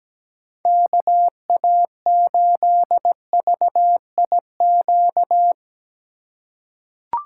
Morse code, heard as a keyed tone, about 17 wpm, 700 Hz. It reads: KA8VIQ
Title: USA Call Signs - 6 Characters - 3x Character Spacing - 17wpm